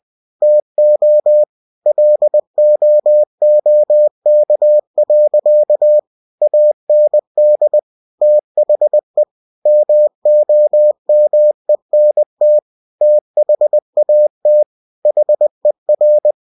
Morse code, heard as a keyed tone, about 20 wpm, 600 Hz. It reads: TO LOOK. AND THE MOMENT THAT HER
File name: cw-books / winnie-the-pooh / winnie-20wpm-page-0069